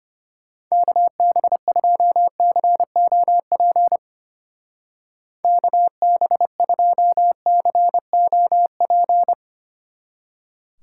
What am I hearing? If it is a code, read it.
Morse code, heard as a keyed tone, about 31 wpm, 700 Hz. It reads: KB2COP KB2COP
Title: Speed Racing - USA 2x3 Call Signs 25wpm